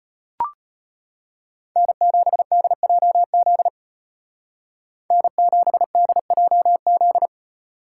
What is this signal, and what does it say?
Morse code, heard as a keyed tone, about 30 wpm, 700 Hz. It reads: N7DJZ N7DJZ
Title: Speed Racing - USA 1x3 Call Signs 25wpm